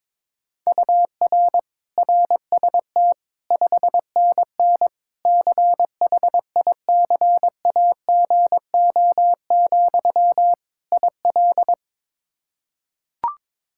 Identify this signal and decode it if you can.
Morse code, heard as a keyed tone, about 22 wpm, 700 Hz. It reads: UR RST 5NN CHICAGO, IL
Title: K1USN SST QSO Practice 22wpm